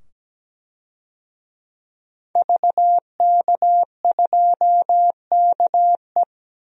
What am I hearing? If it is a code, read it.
Morse code, heard as a keyed tone, about 17 wpm, 700 Hz. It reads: VK2KE
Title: DXCC-100 Call Signs - 5 Characters - 17wpm